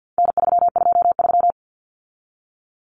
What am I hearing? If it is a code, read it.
Morse code, heard as a keyed tone, about 50 wpm, 700 Hz. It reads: N323
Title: Common Cut Numbers - 4-Digit 50wpm